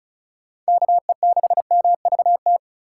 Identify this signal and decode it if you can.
Morse code, heard as a keyed tone, about 36 wpm, 700 Hz. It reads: KE6MVT
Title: USA Call Signs - 6 Characters - 6x Character Spacing - 35wpm